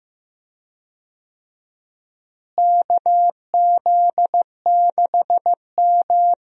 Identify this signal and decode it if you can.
Morse code, heard as a keyed tone, about 15 wpm, 700 Hz. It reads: KZ6M
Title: USA Call Signs - 4 Characters - Repeated 3x - 15wpm